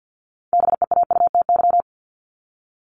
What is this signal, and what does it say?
Morse code, heard as a keyed tone, about 52 wpm, 700 Hz. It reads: BEAUTY
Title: Top 500 Words Repeated 50wpm